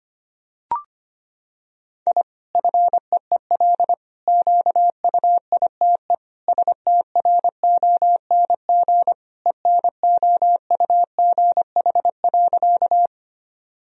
Morse code, heard as a keyed tone, about 25 wpm, 700 Hz. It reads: I FEEL QUITE STRONG ENOUGH.
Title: Sentences from Top 1000 Words - Review Part 3 - 25wpm